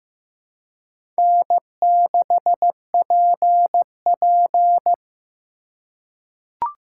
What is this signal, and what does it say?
Morse code, heard as a keyed tone, about 15 wpm, 700 Hz. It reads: N6PP
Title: USA Call Signs - 4 Characters - 4x Character Spacing - 15wpm